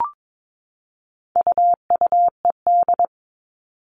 Morse code, heard as a keyed tone, about 22 wpm, 700 Hz. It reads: UUED
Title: All Cut Numbers - 4-Digit 22wpm